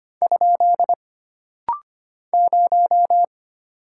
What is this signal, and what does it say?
Morse code, heard as a keyed tone, about 25 wpm, 700 Hz. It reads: ? 0
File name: Single Character (Letter:Number:Punctuation:Prosign) - Rapid-Fire - 25wpm